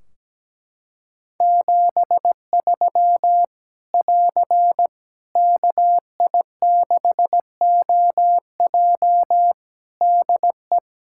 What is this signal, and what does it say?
Morse code, heard as a keyed tone, about 17 wpm, 700 Hz. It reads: 73 <AR> KI6OJ DE
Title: K1USN SST QSO Practice 17wpm